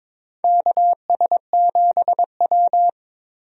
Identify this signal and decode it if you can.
Morse code, heard as a keyed tone, about 22 wpm, 700 Hz. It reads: KS7W
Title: USA Call Signs - 4 Characters - 22wpm